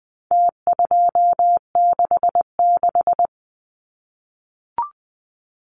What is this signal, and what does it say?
Morse code, heard as a keyed tone, about 20 wpm, 700 Hz. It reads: T266
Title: Numbers - Contest Style - 4-Digit 20wpm